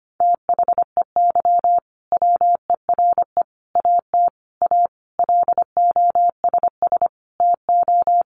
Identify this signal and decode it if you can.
Morse code, heard as a keyed tone, about 25 wpm, 700 Hz. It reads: THEY WERE AT A LOSS TO
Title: Sentences from Top 1400 Words 25wpm